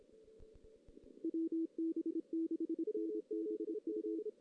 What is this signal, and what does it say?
Morse code, heard as a keyed tone, about 26 wpm, 330 Hz. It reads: WB6NBU
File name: 2016 JAN CW NAQP / 2016 Jan naqp k5ot - 0-3